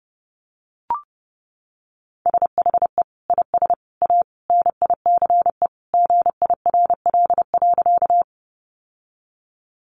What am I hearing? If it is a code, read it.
Morse code, heard as a keyed tone, about 30 wpm, 700 Hz. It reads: SHE IS A NICE GIRL.
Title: Sentences from Top 1100 Words 30wpm